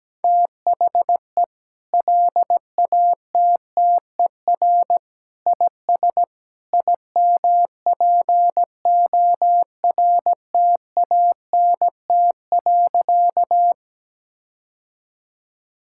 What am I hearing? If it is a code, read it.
Morse code, heard as a keyed tone, about 17 wpm, 700 Hz. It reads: THE LATTER IS IMPORTANT.
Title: Sentences from Top 1600 Words 17wpm